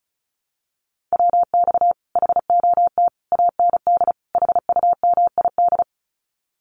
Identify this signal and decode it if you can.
Morse code, heard as a keyed tone, about 36 wpm, 700 Hz. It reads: WX HOT AND HUMID
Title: Common QSO Phrases Encore 35wpm